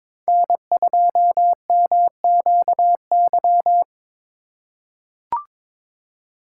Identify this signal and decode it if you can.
Morse code, heard as a keyed tone, about 22 wpm, 700 Hz. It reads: N2MQY
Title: USA Call Signs - 3x Character Spacing - 22wpm